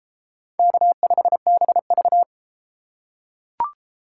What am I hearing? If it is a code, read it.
Morse code, heard as a keyed tone, about 31 wpm, 700 Hz. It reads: K5BV